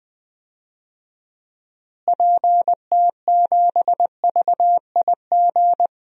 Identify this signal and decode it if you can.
Morse code, heard as a keyed tone, about 20 wpm, 700 Hz. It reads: PT7VIG
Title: DXCC-100 Call Signs - 6 Characters - Repeated 3x - 20wpm